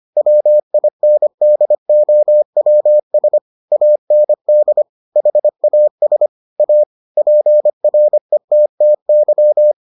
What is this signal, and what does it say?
Morse code, heard as a keyed tone, about 25 wpm, 600 Hz. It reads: WINDOWS AND HAS A PRETTY